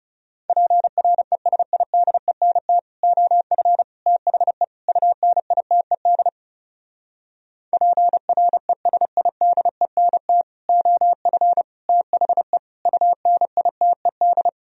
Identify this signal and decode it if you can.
Morse code, heard as a keyed tone, about 38 wpm, 700 Hz. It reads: PRESIDENT OF THE UNITED PRESIDENT OF THE UNITED